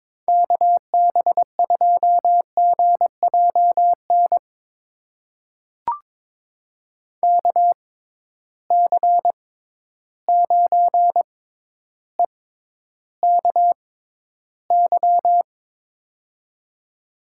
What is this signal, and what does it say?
Morse code, heard as a keyed tone, about 22 wpm, spaced as for 11 wpm, 700 Hz. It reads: KB2GJN KC9EKY
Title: USA Call Signs - 6 Characters - 6x Character Spacing - 22wpm